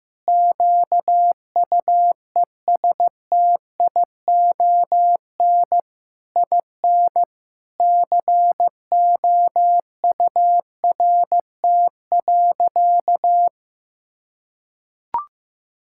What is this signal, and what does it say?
Morse code, heard as a keyed tone, about 15 wpm, 700 Hz. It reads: QUESTION IN COURT.
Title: Sentences from Top 1600 Words 15wpm